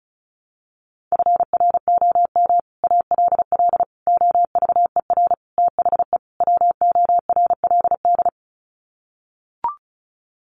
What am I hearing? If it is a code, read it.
Morse code, heard as a keyed tone, about 34 wpm, 700 Hz. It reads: FROM ALL OVER THE WORLD
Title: Sets of 5 Words 35wpm